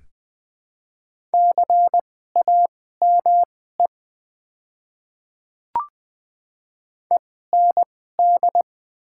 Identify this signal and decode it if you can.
Morse code, heard as a keyed tone, about 20 wpm, spaced as for 14 wpm, 700 Hz. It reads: CAME END
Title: 2x Character Spacing - Top 200 Words - 20wpm